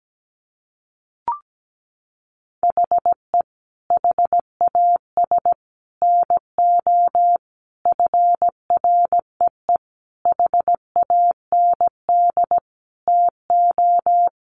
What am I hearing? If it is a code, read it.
Morse code, heard as a keyed tone, about 17 wpm, 700 Hz. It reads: HE HAS NO FREE HAND TO
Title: Sentences from Top 1400 Words 17wpm